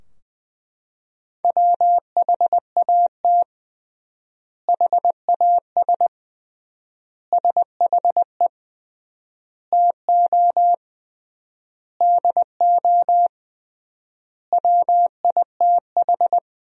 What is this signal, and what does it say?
Morse code, heard as a keyed tone, about 20 wpm, 700 Hz. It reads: WHAT HAS SHE TO DO WITH
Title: 3x Word-Spacing - Sentences from Top 100 Words - 20wpm